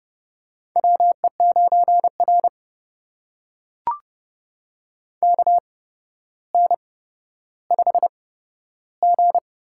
Morse code, heard as a keyed tone, about 30 wpm, spaced as for 13 wpm, 700 Hz. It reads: WE9R KN5G
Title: USA Call Signs - 4 Characters - 8x Character Spacing - 30wpm